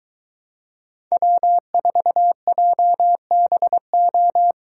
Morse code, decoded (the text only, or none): W4JBO